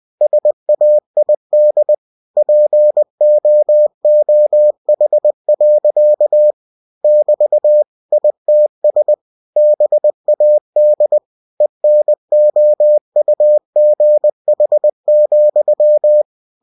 Morse code, heard as a keyed tone, about 20 wpm, 600 Hz. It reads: SAID POOH. <BT> ITS BAD ENOUGH,
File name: cw-books / winnie-the-pooh / winnie-20wpm-page-0048